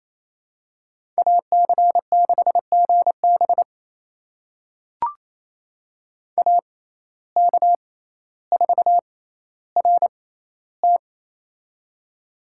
Morse code, heard as a keyed tone, about 28 wpm, spaced as for 14 wpm, 700 Hz. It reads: AC6GB AK4RT
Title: USA Call Signs - 5 Characters - 6x Character Spacing - 28wpm